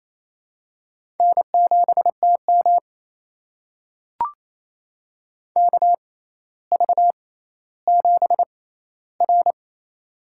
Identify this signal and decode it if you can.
Morse code, heard as a keyed tone, about 28 wpm, spaced as for 14 wpm, 700 Hz. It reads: N7TM KV7R
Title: USA Call Signs - 4 Characters - 6x Character Spacing - 28wpm